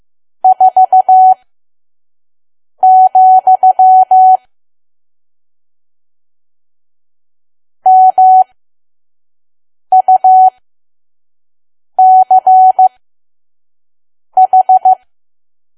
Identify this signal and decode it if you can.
Morse code, heard as a keyed tone, about 15 wpm, spaced as for 5 wpm, 750 Hz. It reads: 4, MUCH
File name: Archive / 5 WPM files / PM